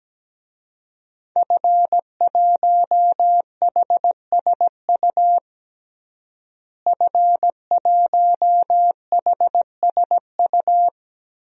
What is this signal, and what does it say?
Morse code, heard as a keyed tone, about 17 wpm, 700 Hz. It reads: F1HSU F1HSU
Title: DXCC-100 Call Signs - 5 Characters - Repeated 2x - 17wpm